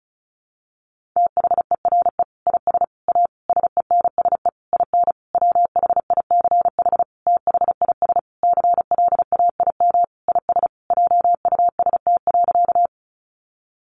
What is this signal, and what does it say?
Morse code, heard as a keyed tone, about 34 wpm, 700 Hz. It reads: THERE IS A SENSE IN WHICH THIS CLAIM IS JUST.
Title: Sentences from Top 1500 Words 35wpm